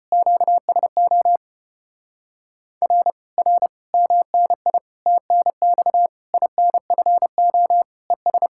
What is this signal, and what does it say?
Morse code, heard as a keyed tone, about 33 wpm, 700 Hz. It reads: QSO R R MNI TNX INFO ES